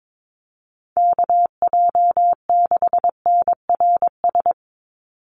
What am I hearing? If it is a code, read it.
Morse code, heard as a keyed tone, about 22 wpm, 700 Hz. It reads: KJ6NRS
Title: USA Call Signs 22wpm